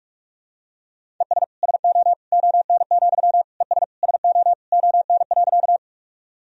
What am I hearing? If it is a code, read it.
Morse code, heard as a keyed tone, about 46 wpm, 700 Hz. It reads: ES SO ON, ES SO ON.
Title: Sentences from Top 100 Words and Abbreviations 45wpm